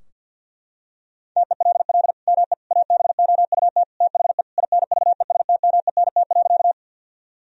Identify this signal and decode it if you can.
Morse code, heard as a keyed tone, about 46 wpm, 700 Hz. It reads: TELL ME ABOUT THE INVESTMENT.
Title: Sentences from Top 1000 Words - Review Part 4 - 50wpm